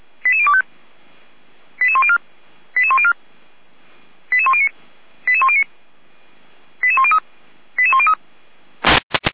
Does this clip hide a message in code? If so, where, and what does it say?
Yes: five-tone selective call (ZVEI) from 0.2 to 8.1 s, heard as quick runs of five tones, tones of about 70 ms. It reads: 88025 88104 88104 88109 88109 88103 88103